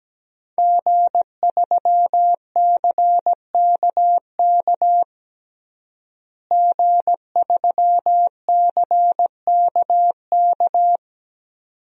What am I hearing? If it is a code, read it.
Morse code, heard as a keyed tone, about 17 wpm, 700 Hz. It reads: G3CKK G3CKK
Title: DXCC-100 Call Signs - 5 Characters - Repeated 2x - 17wpm